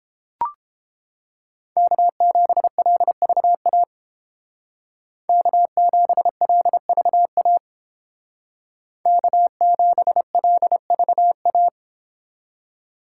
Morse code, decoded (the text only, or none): K7LVA K7LVA K7LVA